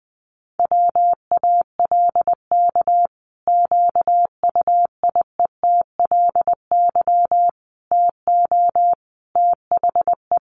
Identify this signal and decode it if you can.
Morse code, heard as a keyed tone, about 20 wpm, 700 Hz. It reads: WALK QUIETLY TO THE
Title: A Sound Of Thunder 20wpm